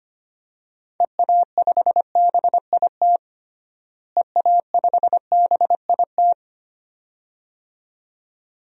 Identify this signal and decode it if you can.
Morse code, heard as a keyed tone, about 25 wpm, 700 Hz. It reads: EA5BIT EA5BIT
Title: DXCC-100 Call Signs - 6 Characters - Repeated 2x - 25wpm